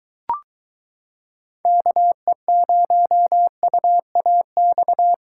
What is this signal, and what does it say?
Morse code, heard as a keyed tone, about 22 wpm, 700 Hz. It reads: KE0UAX